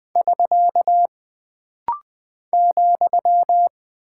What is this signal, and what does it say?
Morse code, heard as a keyed tone, about 20 wpm, 700 Hz. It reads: <SK> ,
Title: Single Punctuation-Prosign - Rapid-Fire - 20wpm